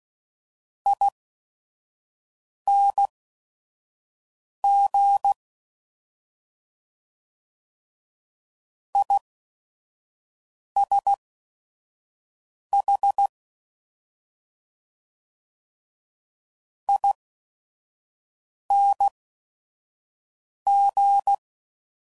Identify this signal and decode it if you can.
Morse code, heard as a keyed tone, about 16 wpm, spaced as for 5 wpm, 800 Hz. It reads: ING ISH ING